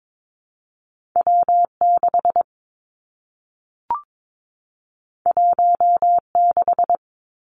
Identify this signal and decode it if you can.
Morse code, heard as a keyed tone, about 22 wpm, 700 Hz. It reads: W 6 1 6
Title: Two Challenging Letter-Number Combinations 22wpm